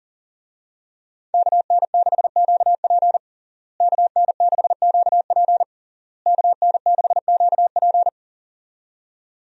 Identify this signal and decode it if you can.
Morse code, heard as a keyed tone, about 40 wpm, 700 Hz. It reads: KN6QP KN6QP KN6QP
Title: USA Call Signs - 5 Characters - Repeated 3x - 40wpm